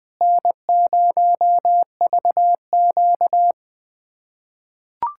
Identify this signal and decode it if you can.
Morse code, heard as a keyed tone, about 20 wpm, 700 Hz. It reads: N0VQ
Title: USA Callsign DE Callsign - 4 Characters - 20wpm